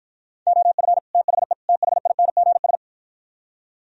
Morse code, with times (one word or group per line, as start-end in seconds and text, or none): 0.5-1.0 OF
1.1-1.5 THE
1.7-2.8 THINGS